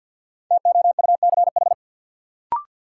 Morse code, heard as a keyed tone, about 47 wpm, 700 Hz. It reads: TOUCH
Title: Sentences from Top 1200 Words - New Word Drill - 50wpm